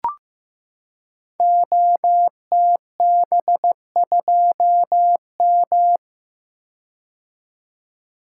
Morse code, heard as a keyed tone, about 15 wpm, 700 Hz. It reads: OTB2M
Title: Code Groups - 5 Letter:Numbers - 15wpm